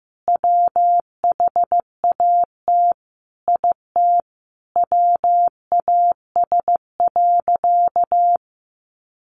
Morse code, heard as a keyed tone, about 14 wpm, 700 Hz. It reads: WHAT IT WAS.
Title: Sentences from Top 1400 Words 15wpm